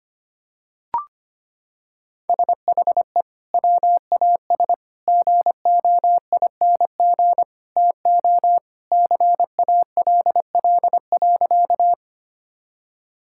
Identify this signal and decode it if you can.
Morse code, heard as a keyed tone, about 25 wpm, 700 Hz. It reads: SHE WAS GOING TO CALL.
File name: practice / 25wpm / Sentences with 3-Word Phrases - 25wpm